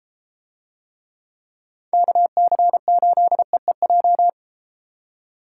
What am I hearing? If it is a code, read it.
Morse code, heard as a keyed tone, about 32 wpm, 700 Hz. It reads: KC8EEJ